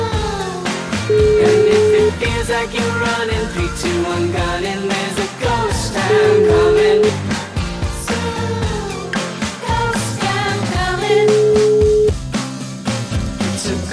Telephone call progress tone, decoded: ringback tone

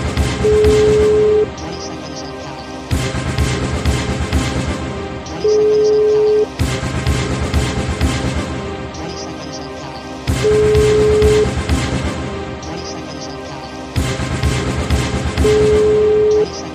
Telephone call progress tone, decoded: ringback tone

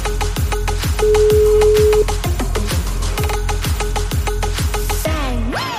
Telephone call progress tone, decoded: ringback tone

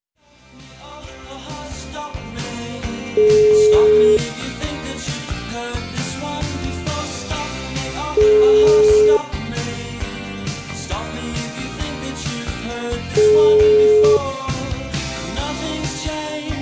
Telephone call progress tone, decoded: ringback tone

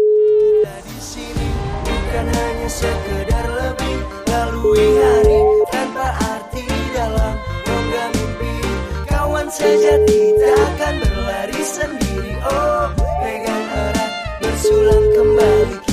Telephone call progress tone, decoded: ringback tone

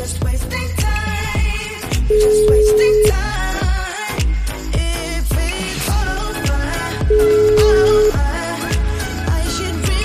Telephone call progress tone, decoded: ringback tone

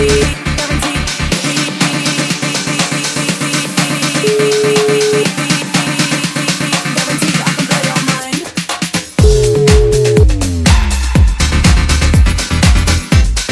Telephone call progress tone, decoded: ringback tone

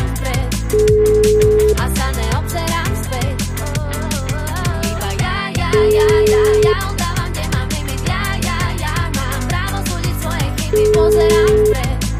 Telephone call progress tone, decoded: ringback tone

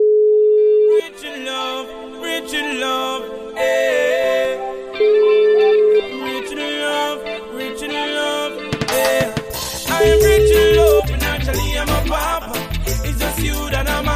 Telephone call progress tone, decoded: ringback tone

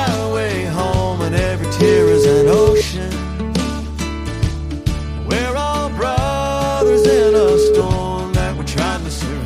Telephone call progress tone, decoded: ringback tone

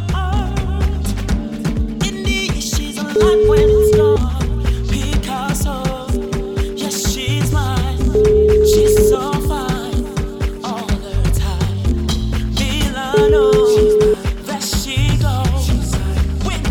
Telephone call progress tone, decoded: ringback tone

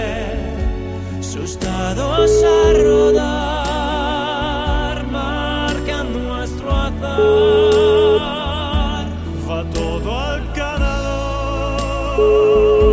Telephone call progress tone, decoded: ringback tone

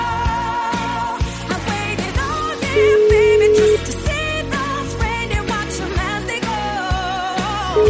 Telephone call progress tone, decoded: ringback tone

